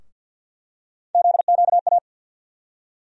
Morse code, heard as a keyed tone, about 48 wpm, 700 Hz. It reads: ZQA
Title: USA Call Sign Postfixes - 3 Letters 50wpm